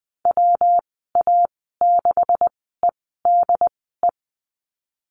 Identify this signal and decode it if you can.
Morse code, heard as a keyed tone, about 20 wpm, spaced as for 15 wpm, 700 Hz. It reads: WA6EDE
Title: 147 Lesson 24g - Letter y - Call Signs 20wpm